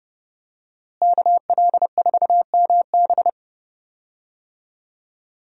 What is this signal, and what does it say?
Morse code, heard as a keyed tone, about 30 wpm, 700 Hz. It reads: KL4MB
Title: USA Call Signs - 5 Characters - Repeated 2x - 30wpm